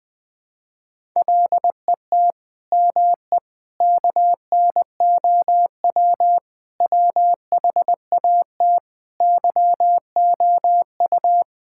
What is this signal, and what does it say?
Morse code, heard as a keyed tone, about 20 wpm, 700 Hz. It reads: LET ME KNOW WHAT YOU
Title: Sentences with 9-Word Phrases - 20wpm